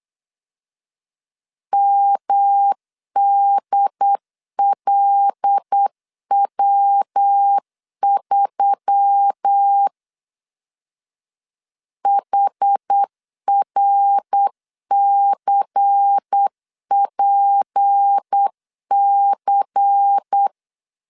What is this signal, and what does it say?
Morse code, heard as a keyed tone, about 8 wpm, 800 Hz. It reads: MDLW3 HRCPC
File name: CW / DET / 1 6 1500C S B 8wpm 5142